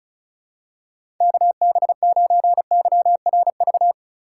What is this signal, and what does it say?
Morse code, heard as a keyed tone, about 36 wpm, 700 Hz. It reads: KB9YRV
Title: USA Call Signs - 6 Characters - 4x Character Spacing - 35wpm